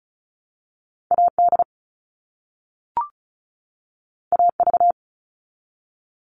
Morse code, heard as a keyed tone, about 36 wpm, 700 Hz. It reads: AD AV
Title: All Cut Numbers - 2-Digit 35wpm